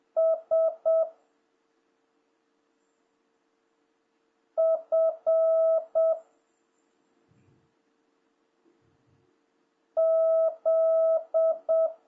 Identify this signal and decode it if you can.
Morse code, heard as a keyed tone, about 7 wpm, 1300 Hz. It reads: S F Z
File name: jh8cbh / 2026cw